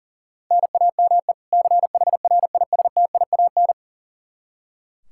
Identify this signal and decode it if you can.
Morse code, heard as a keyed tone, about 40 wpm, 700 Hz. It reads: NAME CHRISTIAN
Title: Practice QSO - Exchange 1 - Beginning - 40wpm